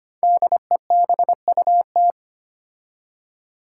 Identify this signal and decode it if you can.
Morse code, heard as a keyed tone, about 25 wpm, 700 Hz. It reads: DEBUT
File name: 5 Letter Words - First 3 Letters Given - 25wpm